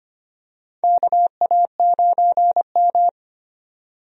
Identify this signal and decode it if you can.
Morse code, heard as a keyed tone, about 25 wpm, 700 Hz. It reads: KA9M
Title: USA Call Signs - 4 Characters - 25wpm